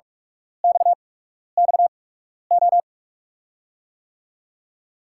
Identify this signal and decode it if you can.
Morse code, heard as a keyed tone, about 46 wpm, spaced as for 12 wpm, 700 Hz. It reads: XXO
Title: USA Call Signs - 5 Characters - 8x Character Spacing - 45wpm